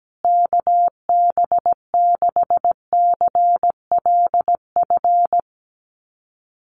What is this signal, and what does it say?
Morse code, heard as a keyed tone, about 17 wpm, 700 Hz. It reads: KB6CLF
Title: USA Call Signs Repeated 3x 17wpm